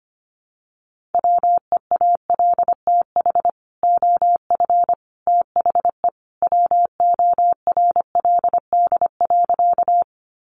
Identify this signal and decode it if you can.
Morse code, heard as a keyed tone, about 25 wpm, 700 Hz. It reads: WEALTH OF THE WORLD.
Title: Sentences from Top 1200 Words 25wpm